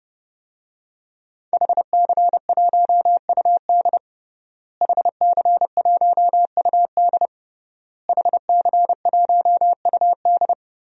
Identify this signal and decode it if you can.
Morse code, heard as a keyed tone, about 30 wpm, 700 Hz. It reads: HC1UD HC1UD HC1UD
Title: DXCC-100 Call Signs - 5 Characters - Repeated 3x - 30wpm